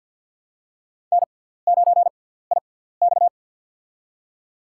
Morse code, heard as a keyed tone, about 50 wpm, spaced as for 17 wpm, 690 Hz. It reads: N9IX
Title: USA Call Signs - 4 Characters - 6x Character Spacing - 50wpm